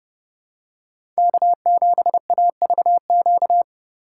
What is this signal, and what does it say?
Morse code, heard as a keyed tone, about 30 wpm, 700 Hz. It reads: K7AVQ
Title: USA Call Signs - 5 Characters - 6x Character Spacing - 30wpm